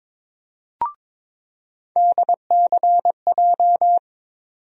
Morse code, heard as a keyed tone, about 22 wpm, 700 Hz. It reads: DCJ
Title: USA Call Sign Postfixes - 3 Letters 22wpm